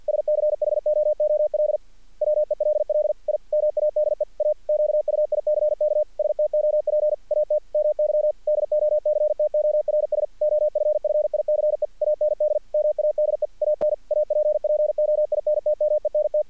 Morse code, heard as a keyed tone, about 48 wpm, 600 Hz. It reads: SCHOOL WELL I MADE A QUICK STOP AT MY DOCTORS OFFICE AND MADE AN APPOINTMENT
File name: archived / CRSnet